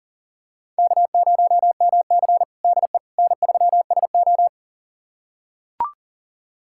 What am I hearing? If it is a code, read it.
Morse code, heard as a keyed tone, about 40 wpm, 700 Hz. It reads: K0MC DE N3SO
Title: USA Callsign DE Callsign - 4 Characters - 40wpm